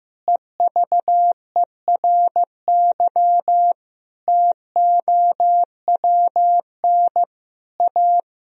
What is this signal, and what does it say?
Morse code, heard as a keyed tone, about 16 wpm, 700 Hz. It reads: EVERY TOWN A
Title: Sentences from Top 1000 Words - Review Part 3 - 15wpm